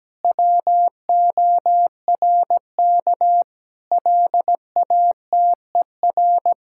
Morse code, heard as a keyed tone, about 17 wpm, 700 Hz. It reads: WORK LATER
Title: Sentences from Top 1000 Words - Review Part 6 - 17wpm